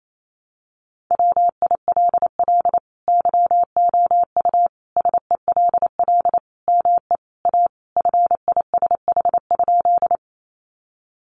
Morse code, heard as a keyed tone, about 28 wpm, 700 Hz. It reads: WILL YOU SELL ME A FISH?
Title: Sentences from Top 1100 Words 28wpm